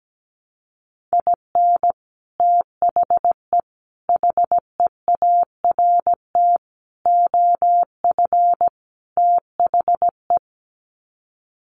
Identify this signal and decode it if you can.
Morse code, heard as a keyed tone, about 17 wpm, 700 Hz. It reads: IN THE HEART OF THE
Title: Sets of 5 Words 17wpm